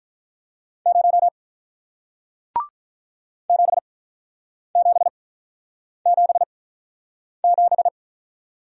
Morse code, heard as a keyed tone, about 48 wpm, spaced as for 9 wpm, 700 Hz. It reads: 0 7777